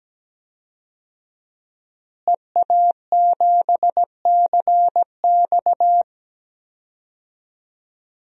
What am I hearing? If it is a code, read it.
Morse code, heard as a keyed tone, about 17 wpm, 700 Hz. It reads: EA7CX
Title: DXCC-100 Call Signs - 5 Characters - Repeated 2x - 17wpm